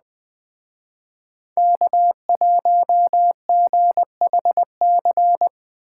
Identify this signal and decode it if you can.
Morse code, heard as a keyed tone, about 20 wpm, 700 Hz. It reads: K1GHC
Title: Speed Racing - USA 1x3 Call Signs 17wpm